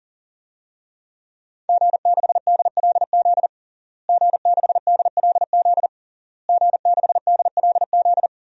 Morse code, heard as a keyed tone, about 40 wpm, 700 Hz. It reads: G6DLZ G6DLZ G6DLZ
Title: DXCC-100 Call Signs - 5 Characters - Repeated 3x - 40wpm